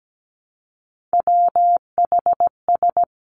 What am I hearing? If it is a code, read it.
Morse code, heard as a keyed tone, about 17 wpm, 700 Hz. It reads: WHS
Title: Three Challenging Letter-Number Combinations 17wpm